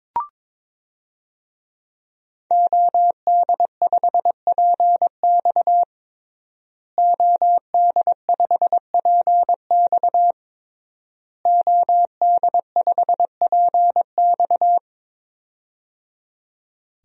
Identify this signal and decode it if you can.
Morse code, heard as a keyed tone, about 22 wpm, 700 Hz. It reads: OD5PX OD5PX OD5PX
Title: DXCC-100 Call Signs - 5 Characters - Repeated 3x - 22wpm